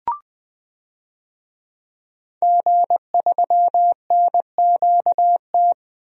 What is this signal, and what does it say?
Morse code, heard as a keyed tone, about 20 wpm, 700 Hz. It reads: G3NQT
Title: DXCC-100 Call Signs - 5 Characters - Repeated 2x - 20wpm